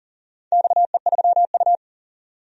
Morse code, heard as a keyed tone, about 40 wpm, 700 Hz. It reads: XE3U
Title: DXCC-100 Call Signs - 4 Characters - 40wpm